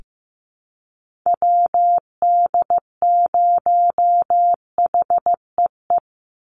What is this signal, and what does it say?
Morse code, heard as a keyed tone, about 15 wpm, 700 Hz. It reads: WD0HEE
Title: USA Call Signs 15wpm